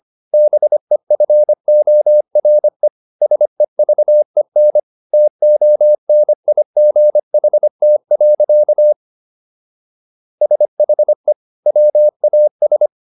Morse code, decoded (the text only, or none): BEFORE SEVEN TONIGHT. SHE WAS